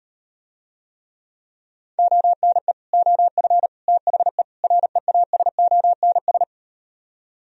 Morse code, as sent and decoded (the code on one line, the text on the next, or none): --- -. / --- ..-. / - .... / .-. .- ... --- -. ...
ONE OF THE REASONS